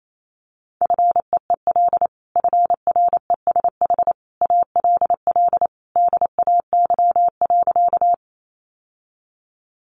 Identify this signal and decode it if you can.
Morse code, heard as a keyed tone, about 28 wpm, 700 Hz. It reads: FEEL FRESH ALL DAY.
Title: Sentences from Top 1200 Words 28wpm